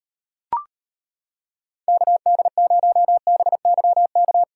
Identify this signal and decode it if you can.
Morse code, heard as a keyed tone, about 38 wpm, 700 Hz. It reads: KD0BYK